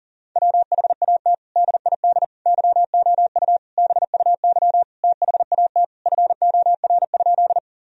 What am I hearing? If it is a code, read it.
Morse code, heard as a keyed tone, about 40 wpm, 700 Hz. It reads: WHAT DID YOU BUY THAT FOR?
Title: Sentences from Top 1000 Words - Review Part 4 - 40wpm